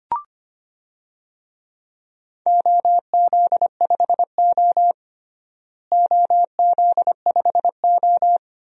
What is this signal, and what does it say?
Morse code, heard as a keyed tone, about 25 wpm, 700 Hz. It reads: OZ5O OZ5O
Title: DXCC-100 Call Signs - 4 Characters - Repeated 2x - 25wpm